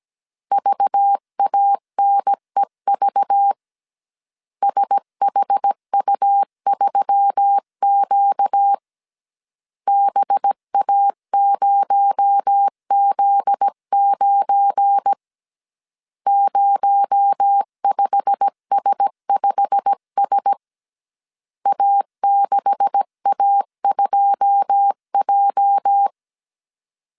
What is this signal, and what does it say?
Morse code, heard as a keyed tone, about 17 wpm, 800 Hz. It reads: VANEV SHU3Q BA0Z9 05S5S A6A2J
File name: 1500C S B 16wpm